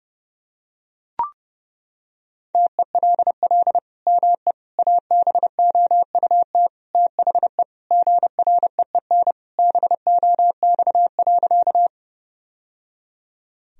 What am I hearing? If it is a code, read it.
Morse code, heard as a keyed tone, about 32 wpm, 700 Hz. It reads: TELL ME ABOUT THE GREEN BOX.
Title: Sentences from Top 1000 Words - Review Part 2 - 30wpm